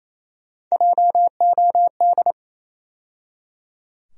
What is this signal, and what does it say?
Morse code, heard as a keyed tone, about 28 wpm, 700 Hz. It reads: JOD
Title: USA Call Sign Postfixes - 3 Letters 28wpm